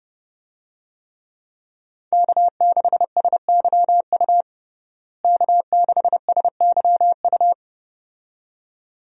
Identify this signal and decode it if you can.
Morse code, heard as a keyed tone, about 30 wpm, 700 Hz. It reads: K6SYU K6SYU
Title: USA Call Signs - 5 Characters - Repeated 2x - 30wpm